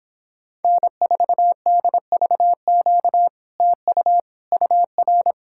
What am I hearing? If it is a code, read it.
Morse code, heard as a keyed tone, about 26 wpm, 700 Hz. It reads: N4DVQ TU UR